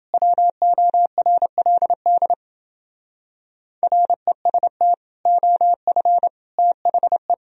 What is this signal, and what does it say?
Morse code, heard as a keyed tone, about 28 wpm, 700 Hz. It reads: WORLD REST OF THE